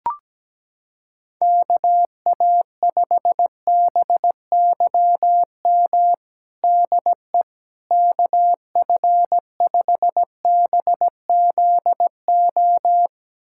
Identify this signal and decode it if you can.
Morse code, heard as a keyed tone, about 17 wpm, 700 Hz. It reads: KA5BYM DE KF5BZO
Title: USA Callsign DE Callsign 17wpm